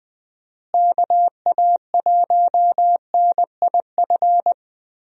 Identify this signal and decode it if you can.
Morse code, heard as a keyed tone, about 21 wpm, 700 Hz. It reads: KA1NIF
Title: USA Call Signs - 6 Characters - Repeated 3x - 20wpm